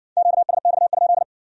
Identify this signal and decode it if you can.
Morse code, heard as a keyed tone, about 57 wpm, 700 Hz. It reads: QSX?